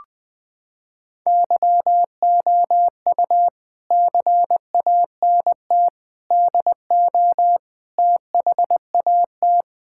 Morse code, heard as a keyed tone, about 20 wpm, 700 Hz. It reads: YOU CANT DO THAT
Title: Sentences with 5-Word Phrases - 20wpm